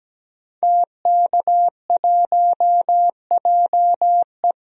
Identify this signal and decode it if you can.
Morse code, heard as a keyed tone, about 17 wpm, 700 Hz. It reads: TK1JE
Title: DXCC-100 Call Signs - 5 Characters - 17wpm